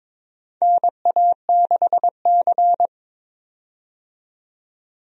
Morse code, heard as a keyed tone, about 22 wpm, 700 Hz. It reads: NA6C
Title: USA Call Signs - 4 Characters - 22wpm